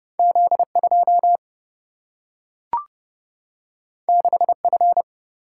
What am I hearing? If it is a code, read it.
Morse code, heard as a keyed tone, about 30 wpm, 700 Hz. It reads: Z2 6F
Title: Single-Letter Single-Number 30wpm